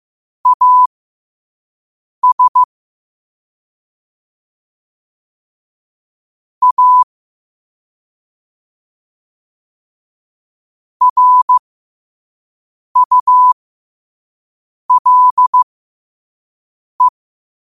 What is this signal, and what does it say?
Morse code, heard as a keyed tone, about 15 wpm, spaced as for 5 wpm, 1000 Hz. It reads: AS A RULE
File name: qotdfiles / 5wpm